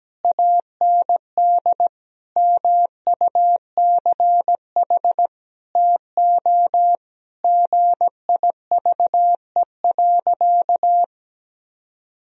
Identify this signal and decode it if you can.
Morse code, heard as a keyed tone, about 17 wpm, 700 Hz. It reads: AND MUCH TO GIVE.
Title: Sentences from Top 1000 Words - Review Part 4 - 17wpm